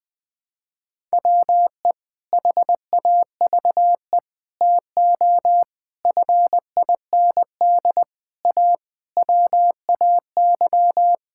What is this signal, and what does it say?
Morse code, heard as a keyed tone, about 20 wpm, 700 Hz. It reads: WE HAVE TO FIND A WAY
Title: Sentences with 6-Word Phrases - 20wpm